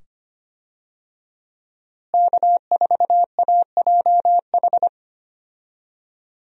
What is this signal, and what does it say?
Morse code, heard as a keyed tone, about 25 wpm, 700 Hz. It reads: K4AJH
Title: USA Call Signs - 5 Characters - Repeated 2x - 25wpm